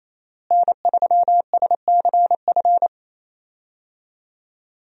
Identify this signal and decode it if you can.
Morse code, heard as a keyed tone, about 28 wpm, 700 Hz. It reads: N3SCF
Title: USA Call Signs - 5 Characters - Repeated 2x - 28wpm